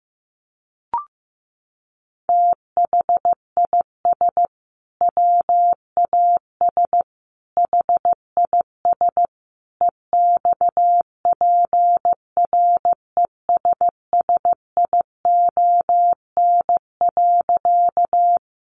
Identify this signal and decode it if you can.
Morse code, heard as a keyed tone, about 14 wpm, 700 Hz. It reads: THIS WAS HIS EXPRESSION.
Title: Sentences from Top 1200 Words 15wpm